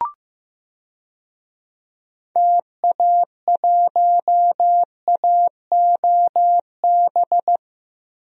Morse code, heard as a keyed tone, about 15 wpm, 700 Hz. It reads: TA1AOB
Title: DXCC-100 Call Signs - 6 Characters - Repeated 2x - 15wpm